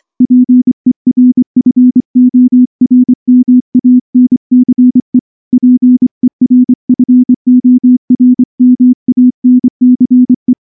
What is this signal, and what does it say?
Morse code, heard as a keyed tone, about 24 wpm, 250 Hz. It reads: PERFORMANCE PERFORMANCE
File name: morse / morse-training-25WPM